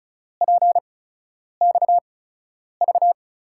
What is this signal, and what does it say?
Morse code, heard as a keyed tone, about 36 wpm, spaced as for 10 wpm, 700 Hz. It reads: PXV
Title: USA Call Signs - 6 Characters - 8x Character Spacing - 35wpm